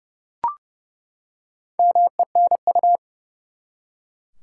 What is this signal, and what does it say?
Morse code, heard as a keyed tone, about 30 wpm, 700 Hz. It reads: MENU
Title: Four Letter Words - 30wpm